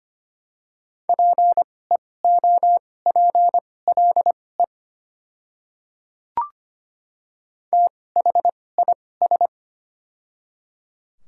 Morse code, heard as a keyed tone, about 25 wpm, spaced as for 18 wpm, 700 Hz. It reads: PEOPLE THIS
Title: 2x Character Spacing - Top 100 Words - 25wpm